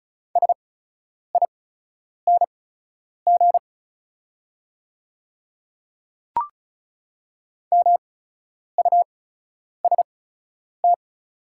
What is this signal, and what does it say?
Morse code, heard as a keyed tone, about 36 wpm, spaced as for 10 wpm, 690 Hz. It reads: SING MUST
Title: 8x Character Spacing - Top 500 Words - 35wpm